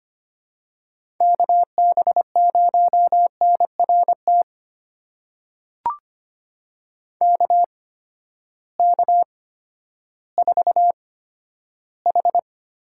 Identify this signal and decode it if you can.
Morse code, heard as a keyed tone, about 25 wpm, spaced as for 12 wpm, 700 Hz. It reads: KB0NRT KK4H
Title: USA Call Signs - 6 Characters - 8x Character Spacing - 25wpm